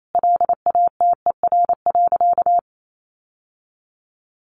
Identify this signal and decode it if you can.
Morse code, heard as a keyed tone, about 28 wpm, 700 Hz. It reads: LATER.
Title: Sentences from Top 700 Words 28wpm